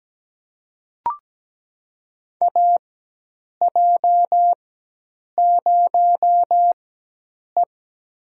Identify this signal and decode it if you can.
Morse code, heard as a keyed tone, about 17 wpm, spaced as for 8 wpm, 700 Hz. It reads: AJ0E
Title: USA Call Signs - 4 Characters - 4x Character Spacing - 17wpm